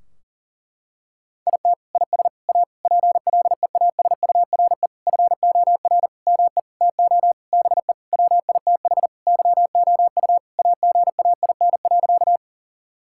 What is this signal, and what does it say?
Morse code, heard as a keyed tone, about 40 wpm, 700 Hz. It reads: IT IS A PLEASURE FOR ME TO BE WITH YOU AGAIN.
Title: Sentences from Top 1600 Words 40wpm